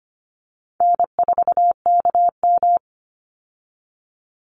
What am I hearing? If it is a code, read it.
Morse code, heard as a keyed tone, about 25 wpm, 700 Hz. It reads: N4KM
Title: Call Signs Repeated 2x - 25wpm